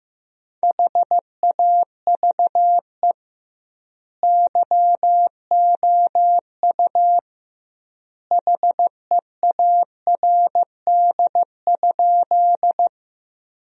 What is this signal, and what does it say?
Morse code, heard as a keyed tone, about 16 wpm, 700 Hz. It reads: HAVE YOU HEARD?
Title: 2x Word-Spacing - Sentences from Top 400 Words - 15wpm